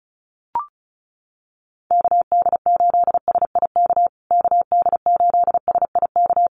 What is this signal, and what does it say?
Morse code, heard as a keyed tone, about 34 wpm, 700 Hz. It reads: KD8SIK KD8SIK
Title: USA Call Signs Repeated 3x 35wpm